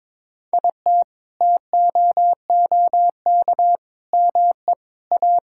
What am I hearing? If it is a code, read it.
Morse code, heard as a keyed tone, about 22 wpm, 700 Hz. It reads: IT TOOK ME A
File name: Sentences with 7-Word Phrases - 22wpm